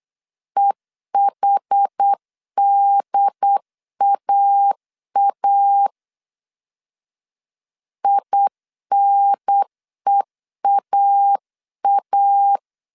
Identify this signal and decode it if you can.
Morse code, heard as a keyed tone, about 8 wpm, 800 Hz. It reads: EHDAA INEAA